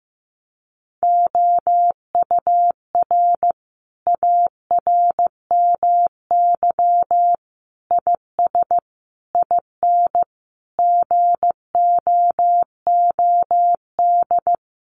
Morse code, heard as a keyed tone, about 15 wpm, 700 Hz. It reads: OUR ARMY IS IN GOOD
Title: Sentences from Top 1500 Words 15wpm